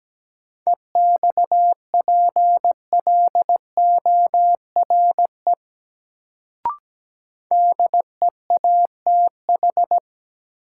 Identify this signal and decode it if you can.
Morse code, heard as a keyed tone, about 17 wpm, 700 Hz. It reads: EXPLORE DEATH
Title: Sentences from Top 1100 Words - New Word Drill - 17wpm